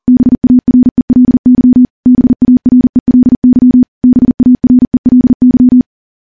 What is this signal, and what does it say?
Morse code, heard as a keyed tone, about 43 wpm, 250 Hz. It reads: BARELY BARELY BARELY
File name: morse / morse-training-40WPM